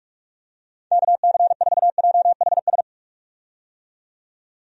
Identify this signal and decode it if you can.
Morse code, heard as a keyed tone, about 46 wpm, 700 Hz. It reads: KC4JHS
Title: USA Call Signs - 6 Characters - 45wpm